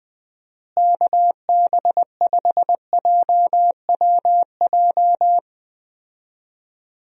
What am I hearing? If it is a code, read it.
Morse code, heard as a keyed tone, about 20 wpm, 700 Hz. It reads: KB5JWJ
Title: USA Call Signs - 6 Characters - Repeated 2x - 20wpm